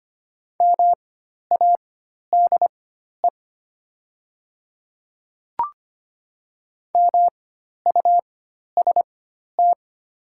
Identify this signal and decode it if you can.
Morse code, heard as a keyed tone, about 25 wpm, spaced as for 12 wpm, 700 Hz. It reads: MADE MUST
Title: 4x Character Spacing - Top 300 Words - 25wpm